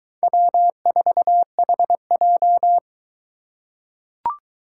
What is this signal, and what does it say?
Morse code, heard as a keyed tone, about 23 wpm, 700 Hz. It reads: W4HJ